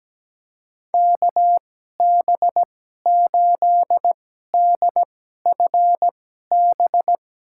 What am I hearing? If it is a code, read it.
Morse code, heard as a keyed tone, about 17 wpm, spaced as for 12 wpm, 700 Hz. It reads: KB8DFB
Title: USA Call Signs - 2x Character Spacing - 17wpm